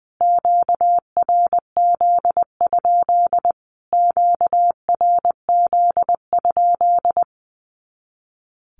Morse code, heard as a keyed tone, about 20 wpm, 700 Hz. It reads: QRZ? QRZ?
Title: Q-Codes Repeated 3x 20wpm